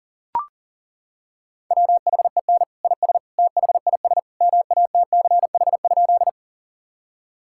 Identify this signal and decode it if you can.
Morse code, heard as a keyed tone, about 40 wpm, 700 Hz. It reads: WHEN IS THIS MATCH?
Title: Sentences from Top 1600 Words 40wpm